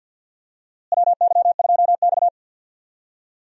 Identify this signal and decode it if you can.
Morse code, heard as a keyed tone, about 47 wpm, 690 Hz. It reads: WY2X
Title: USA Call Signs - 4 Characters - Repeated 2x - 50wpm